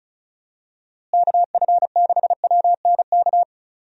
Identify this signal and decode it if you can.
Morse code, heard as a keyed tone, about 36 wpm, 700 Hz. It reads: KF6WNK
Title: USA Call Signs - 6 Characters - Repeated 3x - 35wpm